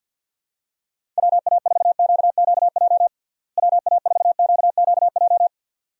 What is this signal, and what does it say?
Morse code, heard as a keyed tone, about 50 wpm, 700 Hz. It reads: WA4QQJ WA4QQJ
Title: USA Call Signs - 6 Characters - Repeated 2x - 50wpm